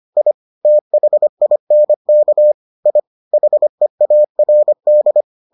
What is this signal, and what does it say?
Morse code, heard as a keyed tone, about 25 wpm, 600 Hz. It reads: I THINK I HEARD